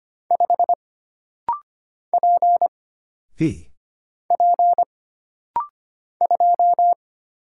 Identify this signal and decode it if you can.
Morse code, heard as a keyed tone, about 25 wpm, 700 Hz. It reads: 5 P P 2
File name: Single Character (Letter:Number:Punctuation:Prosign) - Rapid-Fire - 25wpm